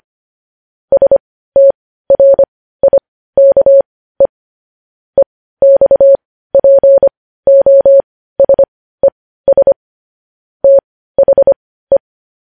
Morse code, heard as a keyed tone, about 25 wpm, spaced as for 15 wpm, 550 Hz. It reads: STRIKE EXPOSES THE
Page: news-headlines-in-morse-code-at-15-wpm